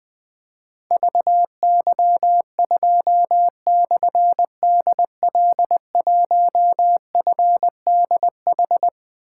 Morse code, heard as a keyed tone, about 20 wpm, 700 Hz. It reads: VY2/DL1FDH
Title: Difficult Call Signs 20wpm